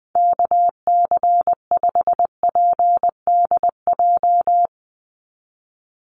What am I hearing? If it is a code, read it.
Morse code, heard as a keyed tone, about 20 wpm, 700 Hz. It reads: KC5PDJ
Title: USA Call Signs 20wpm